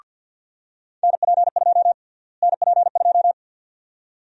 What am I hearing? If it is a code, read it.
Morse code, heard as a keyed tone, about 53 wpm, 700 Hz. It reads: NP2 NP2
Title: USA Call Sign Prefixes - Repeated 2x - 50wpm